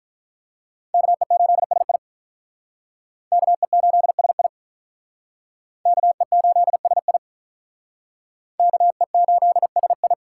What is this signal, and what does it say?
Morse code, heard as a keyed tone, about 44 wpm, 700 Hz. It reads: KE8SI KE8SI KE8SI KE8SI